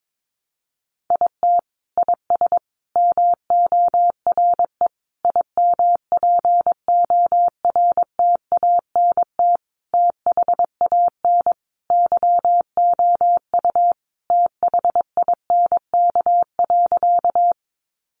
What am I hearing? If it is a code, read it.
Morse code, heard as a keyed tone, about 22 wpm, 700 Hz. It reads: IT IS MORE IMPORTANT THAN YOU THINK.
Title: Sentences from Top 600 Words 22wpm